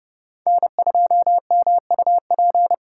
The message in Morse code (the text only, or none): N2MUP